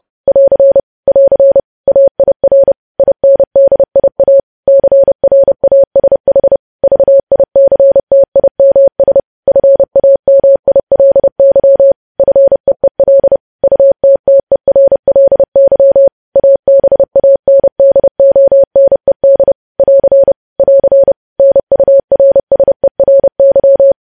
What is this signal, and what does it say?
Morse code, heard as a keyed tone, about 30 wpm, 550 Hz. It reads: <AR> <AR> AIR INDIA CRASH VICTIMS FAMILY FEEL UTTERLY ABANDONED <AR> <AR> NURSERY
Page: news headlines in morse code at 30 wpm